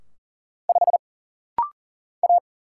Morse code, heard as a keyed tone, about 40 wpm, 700 Hz. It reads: 5 A
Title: Single Letter-Number Alt - Rapid-Fire 40wpm